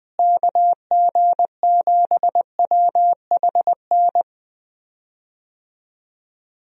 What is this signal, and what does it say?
Morse code, heard as a keyed tone, about 20 wpm, 700 Hz. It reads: KG7WHN
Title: USA Call Signs - 6 Characters - Repeated 2x - 20wpm